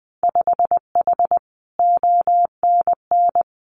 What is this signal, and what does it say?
Morse code, heard as a keyed tone, about 20 wpm, 700 Hz. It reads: 5H ONN
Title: Winter Field Day Exchanges Encore 20wpm